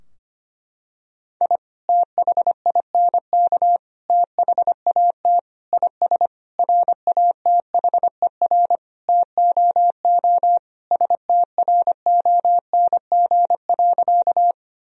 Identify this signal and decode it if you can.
Morse code, heard as a keyed tone, about 25 wpm, 700 Hz. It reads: I THINK THAT IS RATHER TOO STRONG.
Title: Sentences from Top 1000 Words - Review Part 4 - 25wpm